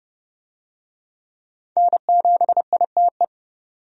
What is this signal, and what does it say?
Morse code, heard as a keyed tone, about 30 wpm, 700 Hz. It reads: N7ITE
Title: USA Call Signs - 5 Characters - 30wpm